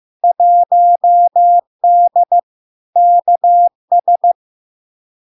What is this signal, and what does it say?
Morse code, heard as a keyed tone, about 15 wpm, 700 Hz. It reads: 1D KS
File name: LARK-FD-2022-Log-in-CW-15wpm-700-Hz